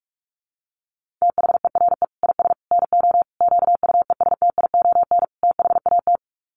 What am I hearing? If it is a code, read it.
Morse code, heard as a keyed tone, about 44 wpm, 700 Hz. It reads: THERE IS NO QUESTION THAT
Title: Sets of 5 Words Encore 45wpm